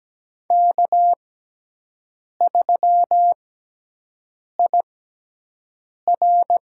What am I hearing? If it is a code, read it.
Morse code, heard as a keyed tone, about 17 wpm, spaced as for 6 wpm, 700 Hz. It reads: K3IR
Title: USA Call Signs - 5 Characters - 6x Character Spacing - 17wpm